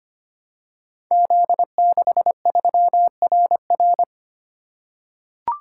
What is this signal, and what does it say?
Morse code, heard as a keyed tone, about 25 wpm, 700 Hz. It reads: Z63RR
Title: Code Groups - 5 Letter:Numbers - 25wpm